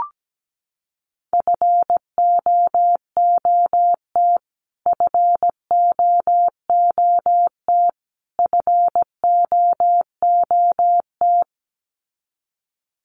Morse code, heard as a keyed tone, about 17 wpm, 700 Hz. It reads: FOOT FOOT FOOT
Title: Top 500 Words Repeated 17wpm